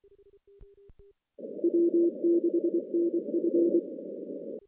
Morse code, heard as a keyed tone, about 25 wpm, 340 Hz. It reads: W6NF
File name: 2016 JAN CW NAQP / 2016 Jan naqp k5ot - 0-10